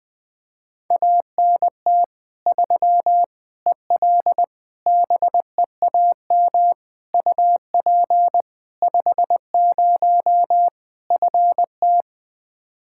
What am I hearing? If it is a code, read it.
Morse code, heard as a keyed tone, about 21 wpm, 700 Hz. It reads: ANT 3 EL BEAM UP 50 FT
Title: Practice QSO - Exchange 2 - Info - 20wpm